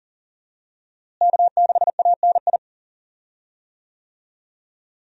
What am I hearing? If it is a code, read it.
Morse code, heard as a keyed tone, about 40 wpm, 700 Hz. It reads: K6ANI
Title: USA Call Signs - 5 Characters - Repeated 3x - 40wpm